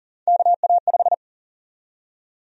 Morse code, heard as a keyed tone, about 40 wpm, 700 Hz. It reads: KA5
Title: USA Call Sign Prefixes - Repeated 2x - 40wpm